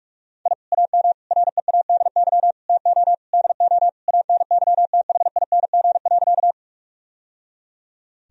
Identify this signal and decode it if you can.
Morse code, heard as a keyed tone, about 41 wpm, 700 Hz. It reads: I AM READY TO DO ANYTHING.
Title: Sentences from Top 1000 Words - Review Part 2 - 45wpm